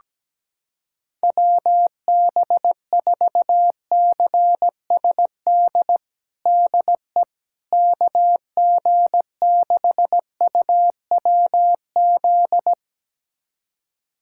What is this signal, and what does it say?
Morse code, heard as a keyed tone, about 17 wpm, 700 Hz. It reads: WB4CSD DE KG6UWZ
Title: USA Callsign DE Callsign 17wpm